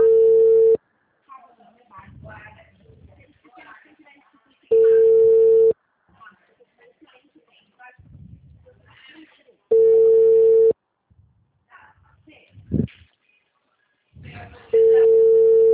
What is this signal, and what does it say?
Ringback tone (China, 450 Hz), with speech.